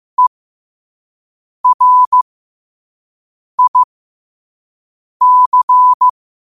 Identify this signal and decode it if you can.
Morse code, heard as a keyed tone, about 15 wpm, spaced as for 5 wpm, 1000 Hz. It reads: ERIC
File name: qotdfiles / 5wpm